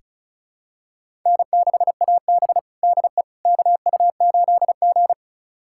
Morse code, heard as a keyed tone, about 36 wpm, 700 Hz. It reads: N6AB DE KU8G
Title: USA Callsign DE Callsign - 4 Characters - 35wpm